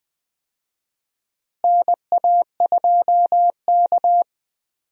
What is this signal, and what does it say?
Morse code, heard as a keyed tone, about 20 wpm, 700 Hz. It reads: NA2K
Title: USA Call Signs - 4 Characters - Repeated 2x - 20wpm